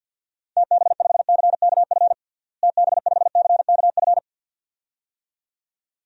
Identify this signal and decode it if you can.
Morse code, heard as a keyed tone, about 51 wpm, 700 Hz. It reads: TB5CKF TB5CKF
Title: DXCC-100 Call Signs - 6 Characters - Repeated 2x - 50wpm